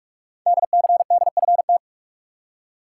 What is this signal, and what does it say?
Morse code, heard as a keyed tone, about 44 wpm, 700 Hz. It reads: DCDFT
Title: Code Groups - 5 Letters - 45wpm